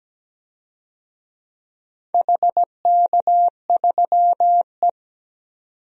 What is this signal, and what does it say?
Morse code, heard as a keyed tone, about 17 wpm, 700 Hz. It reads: HK3E
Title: DXCC-100 Call Signs - 4 Characters - 17wpm